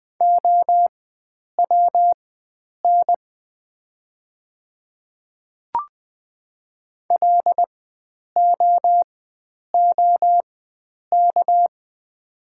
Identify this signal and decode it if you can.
Morse code, heard as a keyed tone, about 20 wpm, spaced as for 9 wpm, 700 Hz. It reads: OWN LOOK
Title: 4x Character Spacing - Top 300 Words - 20wpm